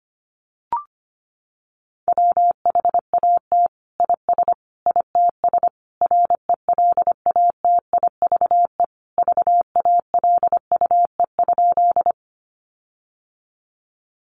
Morse code, heard as a keyed tone, about 25 wpm, 700 Hz. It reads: WHAT IS ITS RELATIVE VALUE?
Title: Sentences from Top 1400 Words 25wpm